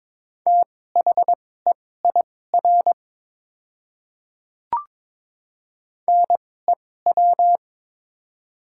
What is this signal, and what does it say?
Morse code, heard as a keyed tone, about 21 wpm, spaced as for 16 wpm, 700 Hz. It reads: THEIR NEW
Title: 2x Character Spacing - Top 200 Words - 22wpm